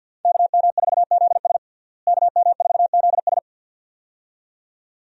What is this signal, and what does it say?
Morse code, heard as a keyed tone, about 46 wpm, 700 Hz. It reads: KM4ZS KM4ZS
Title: USA Call Signs - 5 Characters - Repeated 2x - 50wpm